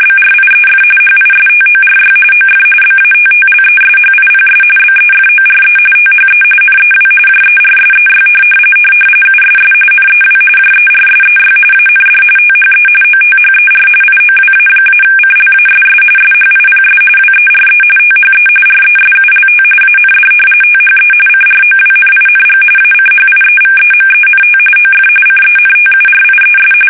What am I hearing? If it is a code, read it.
RTTY, about 45.45 baud. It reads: BRIEF HISTORY OF BATTLESHIPS BEFORE THE NEW JERSEY USS NEW JERSEY AND THE OTHER IOWA-CLASS SHIPS WERE THE RESULT AND THE CROWNING ACHIEVEMENT OF CENTURIES
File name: fsk850 / battleshipnj